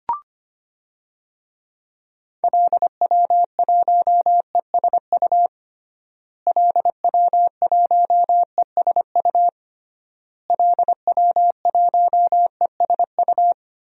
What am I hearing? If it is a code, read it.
Morse code, heard as a keyed tone, about 25 wpm, 700 Hz. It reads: LW1ESU LW1ESU LW1ESU
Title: DXCC-100 Call Signs - 6 Characters - Repeated 3x - 25wpm